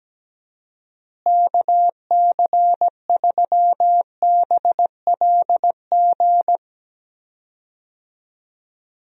Morse code, heard as a keyed tone, about 17 wpm, 700 Hz. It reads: KC3BLG
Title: USA Call Signs - 6 Characters - Repeated 3x - 17wpm